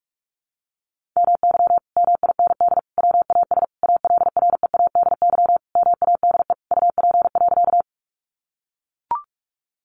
Morse code, heard as a keyed tone, about 44 wpm, 700 Hz. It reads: MY MIND WAS ALREADY MADE UP.
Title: Sentences from Top 700 Words 45wpm